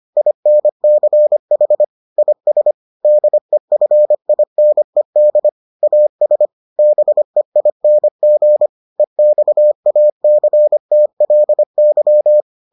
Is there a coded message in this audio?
Yes: Morse code from 0.2 to 12.4 s, heard as a keyed tone, about 25 wpm, 600 Hz. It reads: INCH IS DEFINED AS BEING EXACTLY